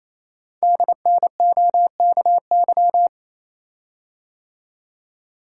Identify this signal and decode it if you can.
Morse code, heard as a keyed tone, about 27 wpm, 700 Hz. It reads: DNOKY
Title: Code Groups - 5 Letters - 28wpm